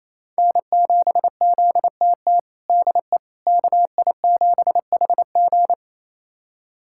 Morse code, heard as a keyed tone, about 28 wpm, 700 Hz. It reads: N7ZTT DE KI7HG
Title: USA Callsign DE Callsign - 5 Characters - 28wpm